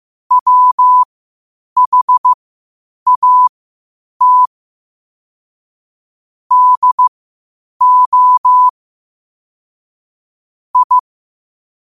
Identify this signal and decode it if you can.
Morse code, heard as a keyed tone, about 15 wpm, spaced as for 8 wpm, 1000 Hz. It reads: WHAT DO I
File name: qotdfiles / 8wpm